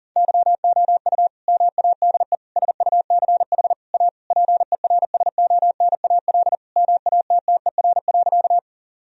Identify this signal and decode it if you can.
Morse code, heard as a keyed tone, about 40 wpm, 700 Hz. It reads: YOU MADE SUCH A PERSONAL MATTER.
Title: Sentences from Top 1000 Words - Review Part 4 - 40wpm